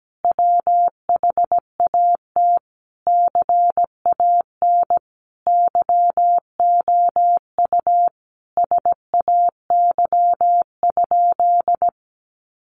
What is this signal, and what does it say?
Morse code, heard as a keyed tone, about 17 wpm, 700 Hz. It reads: WHAT CAN YOU SAY?
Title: Sentences from Top 800 Words 17wpm